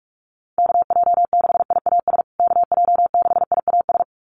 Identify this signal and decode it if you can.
Morse code, heard as a keyed tone, about 44 wpm, 700 Hz. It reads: KJ6IAS KJ6IAS
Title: USA Call Signs Repeated 2x 45wpm